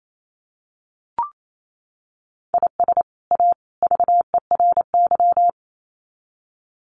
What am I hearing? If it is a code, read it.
Morse code, heard as a keyed tone, about 28 wpm, 700 Hz. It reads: IS A VERY
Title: Sets of 3 Words 28wpm